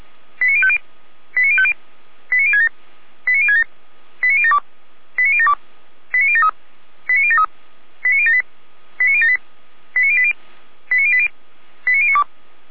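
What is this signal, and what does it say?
Five-tone selective call (ZVEI), heard as quick runs of five tones, tones of about 70 ms. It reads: 89050 89050 89067 89067 89072 89072 89073 89073 89078 89078 89080 89080 89092
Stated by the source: ZVEI five-tone selective call 89011 received with an SDR.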